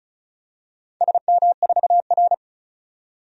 Morse code, heard as a keyed tone, about 36 wpm, 700 Hz. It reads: SM4R
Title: DXCC-100 Call Signs - 4 Characters - 35wpm